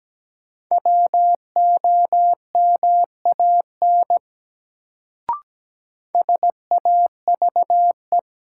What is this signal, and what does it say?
Morse code, heard as a keyed tone, about 17 wpm, 700 Hz. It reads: WOMAN SAVE
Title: Sentences from Top 1100 Words - New Word Drill Alt - 17wpm